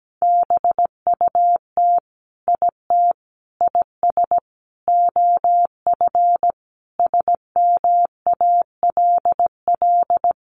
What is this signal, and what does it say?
Morse code, heard as a keyed tone, about 17 wpm, 700 Hz. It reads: BUT IT IS OF SMALL
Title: Sentences from Top 1200 Words 17wpm